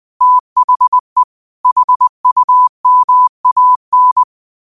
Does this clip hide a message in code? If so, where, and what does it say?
Yes: Morse code from 0.2 to 4.2 s, heard as a keyed tone, about 20 wpm, 1000 Hz. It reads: THE HUMAN